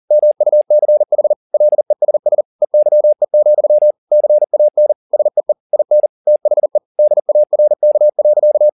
Morse code, heard as a keyed tone, about 40 wpm, 600 Hz. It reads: MUCH LESS EYE, CAN SEE IN THE DARK.